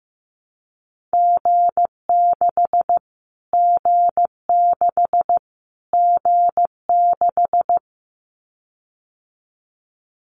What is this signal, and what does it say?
Morse code, heard as a keyed tone, about 15 wpm, 700 Hz. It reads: G6 G6 G6
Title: Callsign Prefixes for Top 25 DXCC - Repeated 3x - 15wpm